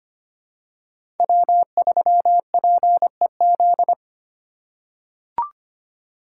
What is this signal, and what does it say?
Morse code, heard as a keyed tone, about 25 wpm, 700 Hz. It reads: W3PEZ
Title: USA Call Signs - 5 Characters - 3x Character Spacing - 25wpm